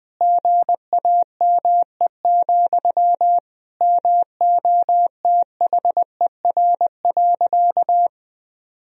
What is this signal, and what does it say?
Morse code, heard as a keyed tone, about 20 wpm, 700 Hz. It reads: GAME, MOTHER.
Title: Sentences from Top 1000 Words - Review Part 3 - 20wpm